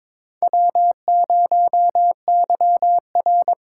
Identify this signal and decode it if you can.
Morse code, heard as a keyed tone, about 22 wpm, 700 Hz. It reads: W0YR
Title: K1USN SST QSO Practice - 22wpm